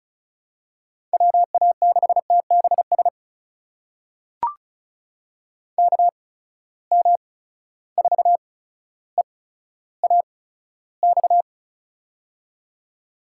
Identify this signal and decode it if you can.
Morse code, heard as a keyed tone, about 34 wpm, spaced as for 15 wpm, 700 Hz. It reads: WA6TBS KM4EAX
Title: USA Call Signs - 6 Characters - 8x Character Spacing - 35wpm